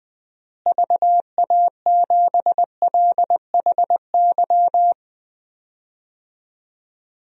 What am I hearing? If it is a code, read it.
Morse code, heard as a keyed tone, about 20 wpm, 700 Hz. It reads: VA7LHY
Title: DXCC-100 Call Signs - 6 Characters - Repeated 3x - 20wpm